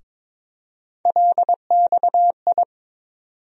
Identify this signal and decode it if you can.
Morse code, heard as a keyed tone, about 22 wpm, 700 Hz. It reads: LXI
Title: USA Call Sign Postfixes - 3 Letters 22wpm